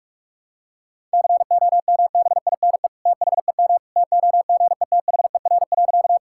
Morse code, heard as a keyed tone, about 46 wpm, 700 Hz. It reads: COMBINE THEM TOGETHER.
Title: Sentences from Top 1600 Words 45wpm